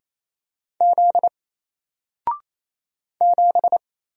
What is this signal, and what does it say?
Morse code, heard as a keyed tone, about 28 wpm, 700 Hz. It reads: Z 7
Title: Single Letter-Number 28wpm